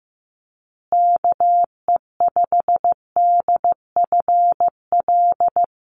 Morse code, heard as a keyed tone, about 15 wpm, 700 Hz. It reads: KE5DFL
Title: USA Call Signs Repeated 3x 15wpm